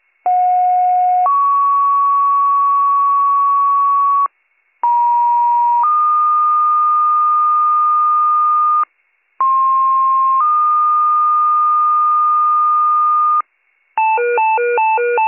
Two-tone page, heard as a one-second tone, then a three-second tone, about 730 then 1120 Hz, about 950 then 1220 Hz, about 1060 then 1220 Hz. It is heard from 0.2 to 13.4 s.